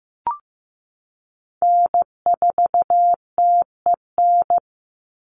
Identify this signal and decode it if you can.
Morse code, heard as a keyed tone, about 15 wpm, 700 Hz. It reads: N4TEN
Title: USA Call Signs 15wpm